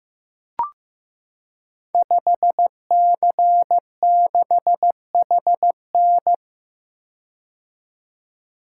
Morse code, heard as a keyed tone, about 15 wpm, 700 Hz. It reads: 5C6HN
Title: Code Groups - 5 Letter:Numbers - 15wpm